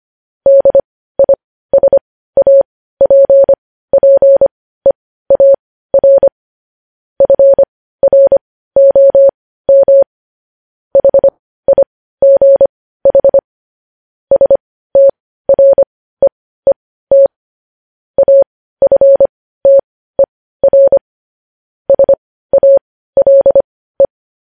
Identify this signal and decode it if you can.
Morse code, heard as a keyed tone, about 25 wpm, spaced as for 15 wpm, 550 Hz. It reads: DISAPPEAR FROM HIGH STREET AFTER SALE
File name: podcast / News headlines-1x-25x15